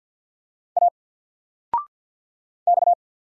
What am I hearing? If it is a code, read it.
Morse code, heard as a keyed tone, about 50 wpm, 700 Hz. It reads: A X
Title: Single Letter-Number 50wpm